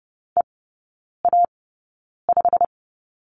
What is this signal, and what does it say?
Morse code, heard as a keyed tone, about 30 wpm, spaced as for 9 wpm, 700 Hz. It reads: EA5
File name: course / 30:10wpm / 087 Lesson 15e - Number 5 - Three Characters 30wpm